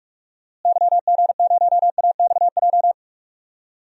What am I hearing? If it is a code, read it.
Morse code, heard as a keyed tone, about 46 wpm, 700 Hz. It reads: YG0AXJ
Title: DXCC-100 Call Signs - 6 Characters - Repeated 3x - 45wpm